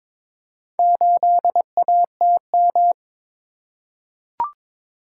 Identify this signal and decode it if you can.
Morse code, heard as a keyed tone, about 22 wpm, 690 Hz. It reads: 8ATM
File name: practice / 22wpm / Four Letter-Number Combinations - 22wpm